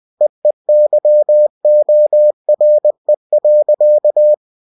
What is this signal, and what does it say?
Morse code, heard as a keyed tone, about 20 wpm, 600 Hz. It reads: EEYORE.